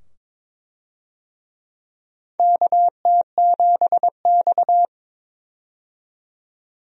Morse code, heard as a keyed tone, about 22 wpm, 700 Hz. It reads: KT7X
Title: USA Call Signs - 4 Characters - 22wpm